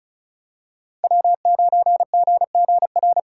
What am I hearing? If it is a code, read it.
Morse code, heard as a keyed tone, about 34 wpm, 700 Hz. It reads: W9GGR
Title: USA Call Signs - 5 Characters - 6x Character Spacing - 35wpm